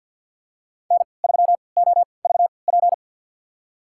Morse code, heard as a keyed tone, about 47 wpm, spaced as for 28 wpm, 700 Hz. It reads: N3OVP
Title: USA Call Signs - 3x Character Spacing - 50wpm